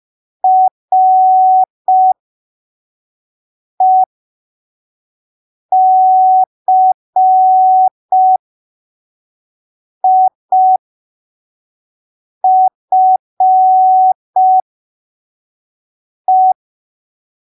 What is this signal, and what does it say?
Morse code, heard as a keyed tone, about 5 wpm, spaced as for 3 wpm, 750 Hz. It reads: RECIFE
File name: arquivos / texto / texto 16 05ppm